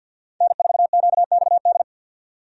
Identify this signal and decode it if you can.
Morse code, heard as a keyed tone, about 52 wpm, 700 Hz. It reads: N4QXD
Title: USA Call Signs - 5 Characters - 50wpm